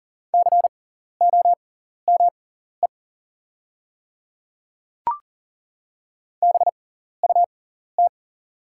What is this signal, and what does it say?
Morse code, heard as a keyed tone, about 40 wpm, spaced as for 14 wpm, 690 Hz. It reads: COME BUT
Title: 6x Character Spacing - Top 200 Words - 40wpm